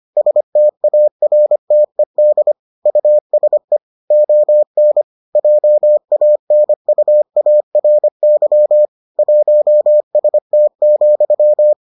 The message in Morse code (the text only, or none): STARTED USE ON JANUARY 1ST,